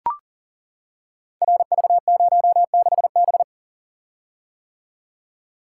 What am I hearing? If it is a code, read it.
Morse code, heard as a keyed tone, about 39 wpm, 700 Hz. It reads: RV06B
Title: Code Groups - 5 Letter:Numbers - 40wpm